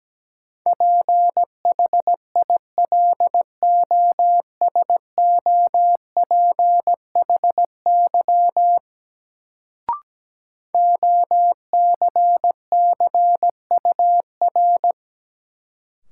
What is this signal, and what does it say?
Morse code, heard as a keyed tone, about 17 wpm, 700 Hz. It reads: PHILOSOPHY OCCUR
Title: Sentences from Top 900 Words - New Word Drill - 17wpm